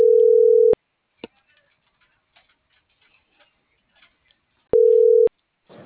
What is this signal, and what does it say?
Ringback tone (North America), 440 + 480 Hz.